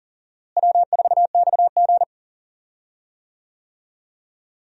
Morse code, heard as a keyed tone, about 40 wpm, 700 Hz. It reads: W4XG
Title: USA Call Signs - 4 Characters - Repeated 2x - 40wpm